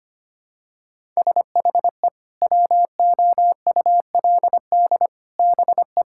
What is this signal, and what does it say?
Morse code, heard as a keyed tone, about 25 wpm, 700 Hz. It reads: SHE WOULD BE
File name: practice / 25wpm / Sentences with 4-Word Phrases - 25wpm